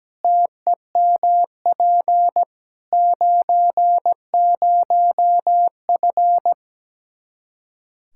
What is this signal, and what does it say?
Morse code, heard as a keyed tone, about 17 wpm, 700 Hz. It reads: TEMP 90F
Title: Practice QSO - Exchange 2 - Info - 17wpm